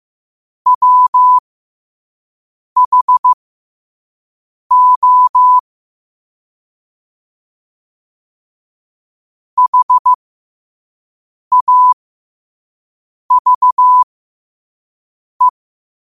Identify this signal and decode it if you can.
Morse code, heard as a keyed tone, about 15 wpm, spaced as for 5 wpm, 1000 Hz. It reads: WHO HAVE
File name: qotdfiles / 5wpm